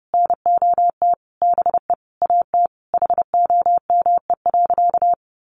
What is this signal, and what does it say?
Morse code, heard as a keyed tone, about 30 wpm, 700 Hz. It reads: NOT BE AT HOME.
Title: Sentences from Top 1500 Words 30wpm